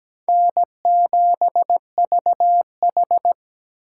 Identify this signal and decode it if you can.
Morse code, heard as a keyed tone, about 17 wpm, 700 Hz. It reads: N7VH
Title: USA Call Signs - 4 Characters - 3x Character Spacing - 17wpm